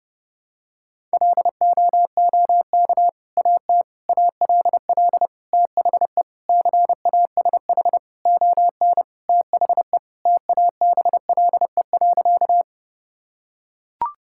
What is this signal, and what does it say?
Morse code, heard as a keyed tone, about 32 wpm, 700 Hz. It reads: LOOK AT ALL THE CASH ON THE TABLE.
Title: Sentences from Top 1000 Words - Review Part 5 - 30wpm